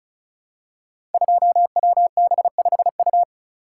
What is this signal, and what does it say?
Morse code, heard as a keyed tone, about 36 wpm, 700 Hz. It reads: 2WB5U
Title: Code Groups - 5 Letter:Numbers - 35wpm